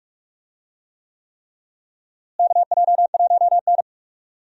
Morse code, heard as a keyed tone, about 46 wpm, 700 Hz. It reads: KJ1N
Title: USA Call Signs - 4 Characters - Repeated 3x - 45wpm